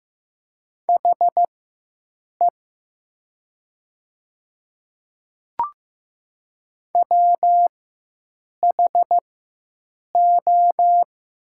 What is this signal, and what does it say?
Morse code, heard as a keyed tone, about 15 wpm, spaced as for 7 wpm, 700 Hz. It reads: HE WHO
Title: 4x Character Spacing - Top 100 Words - 15wpm